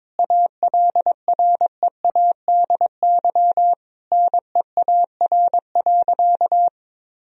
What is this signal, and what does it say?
Morse code, heard as a keyed tone, about 22 wpm, 700 Hz. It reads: ALREADY NEAR.
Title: Sentences from Top 1000 Words - Review Part 1 - 22wpm